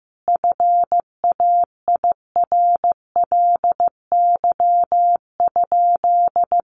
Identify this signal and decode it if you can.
Morse code, heard as a keyed tone, about 15 wpm, 700 Hz. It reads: FAIRLY?
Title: Sentences from Top 1100 Words 15wpm